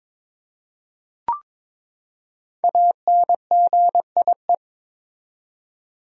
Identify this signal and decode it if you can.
Morse code, heard as a keyed tone, about 22 wpm, 700 Hz. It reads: ANGIE
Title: USA Common Names - 22wpm